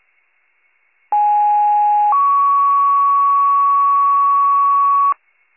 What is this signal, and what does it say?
Two-tone page, heard as a one-second tone, then a three-second tone, about 830 then 1120 Hz. It is heard from 1.1 to 5.1 s.